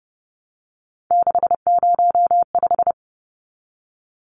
Morse code, heard as a keyed tone, about 30 wpm, 700 Hz. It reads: 605
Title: Numbers - 3-Digit 30wpm